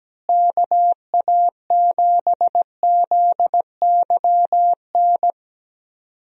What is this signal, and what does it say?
Morse code, heard as a keyed tone, about 17 wpm, 700 Hz. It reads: KA7ZYN
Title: USA Call Signs - 6 Characters - Repeated 2x - 17wpm